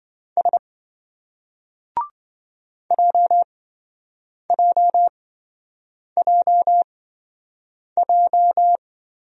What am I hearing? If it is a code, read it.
Morse code, heard as a keyed tone, about 27 wpm, spaced as for 7 wpm, 700 Hz. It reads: S JJJJ